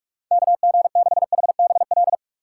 Morse code, heard as a keyed tone, about 42 wpm, 700 Hz. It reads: KG6HBL
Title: USA Call Signs - 3x Character Spacing - 45wpm